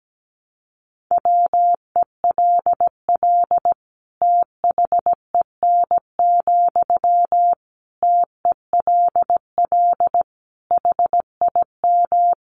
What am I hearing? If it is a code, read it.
Morse code, heard as a keyed tone, about 17 wpm, 700 Hz. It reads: WELL THEN, TELL HIM
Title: Sentences from Top 200 Words 17wpm